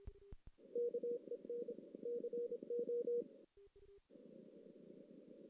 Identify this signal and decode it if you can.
Morse code, heard as a keyed tone, about 26 wpm, 480 Hz. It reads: KEN CO
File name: 2016 JAN CW NAQP / 2016 Jan naqp k5ot - 0-5